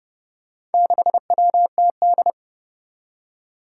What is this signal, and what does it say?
Morse code, heard as a keyed tone, about 30 wpm, 700 Hz. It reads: 6WTD
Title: Four Challenging Letter-Number Combinations 30wpm